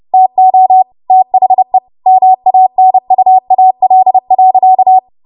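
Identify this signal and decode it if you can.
Morse code, heard as a keyed tone, about 30 wpm, 750 Hz. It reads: TO THE MANUAL.
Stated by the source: Morse practice transmission at 30 WPM.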